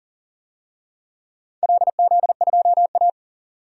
Morse code, heard as a keyed tone, about 41 wpm, 700 Hz. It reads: LZ2A
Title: DXCC-100 Call Signs - 4 Characters - 40wpm